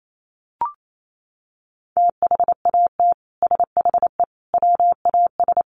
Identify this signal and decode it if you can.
Morse code, heard as a keyed tone, about 28 wpm, 700 Hz. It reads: THAT SHE WAS
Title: Sets of 5 Words Encore2 28wpm